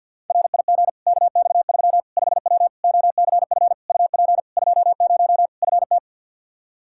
Morse code, heard as a keyed tone, about 49 wpm, 700 Hz. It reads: RIG KX3 5W OCF UP 20 FT
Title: Practice QSO - Exchange 2 - Info - 50wpm